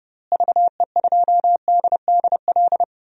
Morse code, heard as a keyed tone, about 30 wpm, 700 Hz. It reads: VE2DDL
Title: DXCC-100 Call Signs - 6 Characters - Repeated 2x - 30wpm